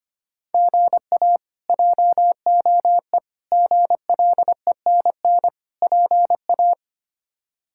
Morse code, heard as a keyed tone, about 25 wpm, 700 Hz. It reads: GA JOE GLENN PA
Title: K1USN SST QSO Practice - 25wpm